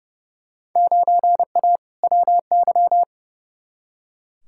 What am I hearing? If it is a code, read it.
Morse code, heard as a keyed tone, about 30 wpm, 700 Hz. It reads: 9A WY
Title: Field Day Exchanges Encore - 30wpm